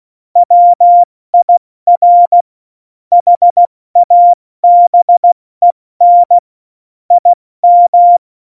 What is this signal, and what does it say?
Morse code, heard as a keyed tone, about 16 wpm, 700 Hz. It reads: WIR HABEN IM
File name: Morsekurs / Witze / Witz 15. Januar 2019, 70 F80